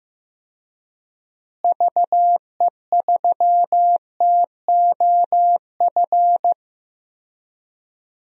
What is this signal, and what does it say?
Morse code, heard as a keyed tone, about 15 wpm, 700 Hz. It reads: VE3TOF
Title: DXCC-100 Call Signs - 6 Characters - Repeated 3x - 15wpm